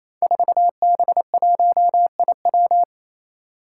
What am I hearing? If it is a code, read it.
Morse code, heard as a keyed tone, about 28 wpm, 700 Hz. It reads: 4B1IW
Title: Five Challenging Letter-Number Combinations 28wpm